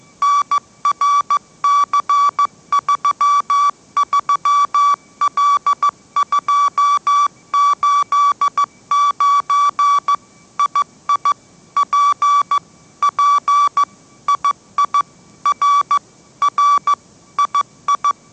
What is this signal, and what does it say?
Morse code, heard as a keyed tone, about 15 wpm, 1150 Hz. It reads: NRC33L289 II P P II R R II
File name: mrx / mrx065 16wpm